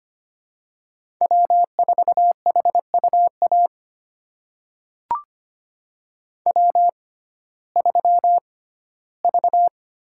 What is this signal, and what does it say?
Morse code, heard as a keyed tone, about 25 wpm, spaced as for 15 wpm, 700 Hz. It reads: W4HUA W3V
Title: USA Call Signs - 6x Character Spacing - 25wpm